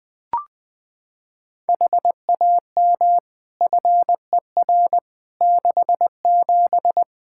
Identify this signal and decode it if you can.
Morse code, heard as a keyed tone, about 20 wpm, 700 Hz. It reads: HAM FER 67
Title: Practice QSO - Exchange 3 - More Info - 20wpm